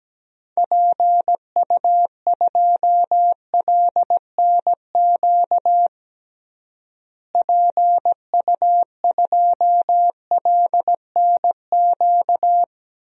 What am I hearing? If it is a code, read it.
Morse code, heard as a keyed tone, about 17 wpm, 700 Hz. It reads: PU2LNQ PU2LNQ
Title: DXCC-100 Call Signs - 6 Characters - Repeated 2x - 17wpm